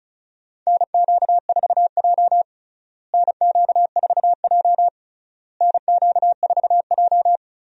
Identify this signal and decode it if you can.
Morse code, heard as a keyed tone, about 36 wpm, 700 Hz. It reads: NQ4J NQ4J NQ4J
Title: USA Call Signs - 4 Characters - Repeated 3x - 35wpm